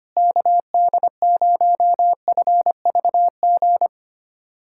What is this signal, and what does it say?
Morse code, heard as a keyed tone, about 25 wpm, 700 Hz. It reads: KD0FVG
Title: USA Callsign DE Callsign - 6 Characters - 25wpm